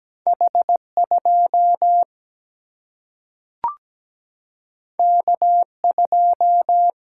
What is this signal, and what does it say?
Morse code, heard as a keyed tone, about 17 wpm, 700 Hz. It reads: H2 K2
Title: Single-Letter Single-Number 17wpm